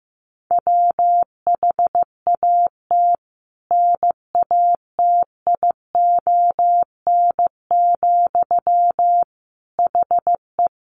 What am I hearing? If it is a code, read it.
Morse code, heard as a keyed tone, about 15 wpm, 700 Hz. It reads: WHAT NATION, HE
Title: Sentences from Top 900 Words 15wpm